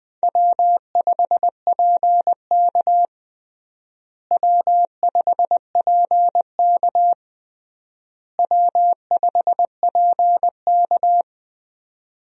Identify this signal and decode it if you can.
Morse code, heard as a keyed tone, about 21 wpm, 700 Hz. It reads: W5PK W5PK W5PK
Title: USA Call Signs - 4 Characters - Repeated 3x - 20wpm